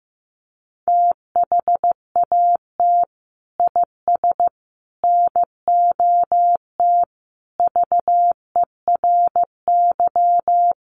Morse code, heard as a keyed tone, about 15 wpm, 700 Hz. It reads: THAT IS NOT VERY
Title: Sentences from Top 600 Words 15wpm